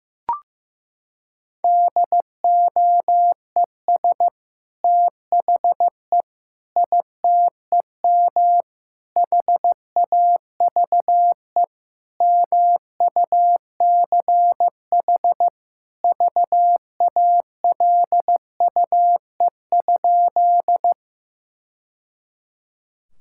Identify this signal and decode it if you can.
Morse code, heard as a keyed tone, about 16 wpm, 700 Hz. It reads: DOES THE ITEM HAVE MUCH VALUE?
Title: Sentences from Top 1000 Words - Review Part 6 - 15wpm